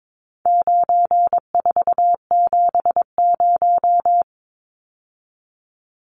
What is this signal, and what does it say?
Morse code, heard as a keyed tone, about 22 wpm, 700 Hz. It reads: 9470
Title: Numbers - 4-Digit 22wpm